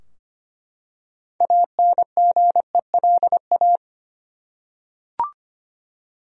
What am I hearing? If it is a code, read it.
Morse code, heard as a keyed tone, about 25 wpm, 700 Hz. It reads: ANGELA
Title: USA Common Names - 25wpm